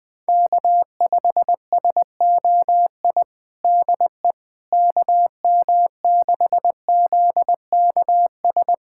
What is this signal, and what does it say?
Morse code, heard as a keyed tone, about 20 wpm, 700 Hz. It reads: K5SOI DE KM6ZKS
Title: USA Callsign DE Callsign 20wpm